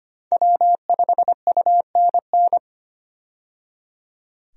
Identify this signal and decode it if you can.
Morse code, heard as a keyed tone, about 25 wpm, 700 Hz. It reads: W5UNN
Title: USA Call Signs - 5 Characters - 25wpm